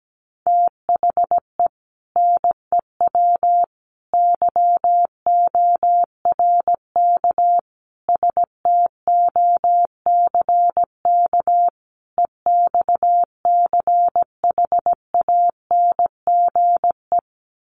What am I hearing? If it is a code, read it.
Morse code, heard as a keyed tone, about 17 wpm, 700 Hz. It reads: THE NEW YORK STOCK EXCHANGE
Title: Sets of 5 Words 17wpm